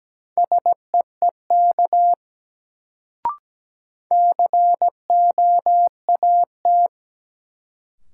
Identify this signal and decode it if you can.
Morse code, heard as a keyed tone, about 17 wpm, 700 Hz. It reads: SEEK COAT
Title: Sentences from Top 1600 Words - New Word Drill - 17wpm